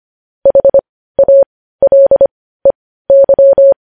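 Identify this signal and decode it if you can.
Morse code, heard as a keyed tone, about 25 wpm, spaced as for 15 wpm, 550 Hz. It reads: HALEY
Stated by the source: Morse code news headlines at 15.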